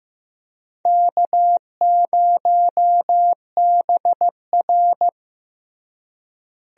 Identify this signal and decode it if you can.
Morse code, heard as a keyed tone, about 15 wpm, 700 Hz. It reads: K0BR
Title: USA Call Signs - 4 Characters - Repeated 2x - 15wpm